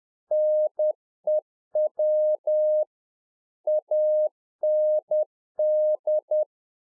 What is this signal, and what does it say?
Morse code, heard as a keyed tone, about 10 wpm, 610 Hz. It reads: NEW AND
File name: Czech Army Morse Keys 10WPM